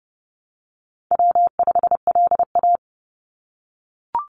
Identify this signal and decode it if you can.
Morse code, heard as a keyed tone, about 30 wpm, 700 Hz. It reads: W5LA
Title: Call Signs - 30wpm